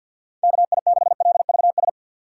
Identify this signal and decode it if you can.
Morse code, heard as a keyed tone, about 53 wpm, 700 Hz. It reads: KI6LVS
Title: USA Call Signs - 3x Character Spacing - 50wpm